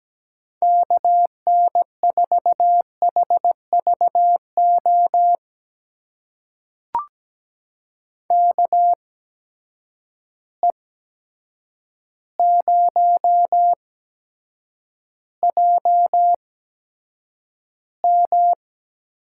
Morse code, heard as a keyed tone, about 17 wpm, spaced as for 8 wpm, 700 Hz. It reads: KN4HVO KE0JM
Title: USA Call Signs - 6 Characters - 8x Character Spacing - 17wpm